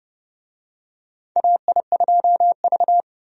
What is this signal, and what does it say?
Morse code, heard as a keyed tone, about 30 wpm, 700 Hz. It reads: AI2V
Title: USA Call Signs - 4 Characters - Repeated 3x - 30wpm